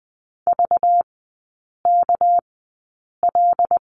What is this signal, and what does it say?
Morse code, heard as a keyed tone, about 20 wpm, spaced as for 8 wpm, 700 Hz. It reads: VKL
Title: 198 Lesson 32e - Letter k - Three Characters 20wpm